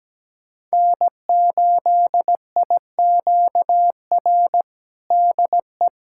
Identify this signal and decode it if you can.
Morse code, heard as a keyed tone, about 17 wpm, 700 Hz. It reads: N8IQR DE
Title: USA Callsign DE Callsign - 5 Characters - 17wpm